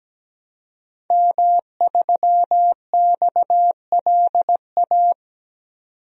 Morse code, heard as a keyed tone, about 17 wpm, 700 Hz. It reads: M3XLA
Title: DXCC-100 Call Signs - 5 Characters - 17wpm